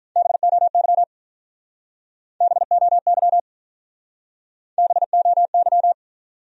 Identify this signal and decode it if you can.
Morse code, heard as a keyed tone, about 47 wpm, 700 Hz. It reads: BOY BOY BOY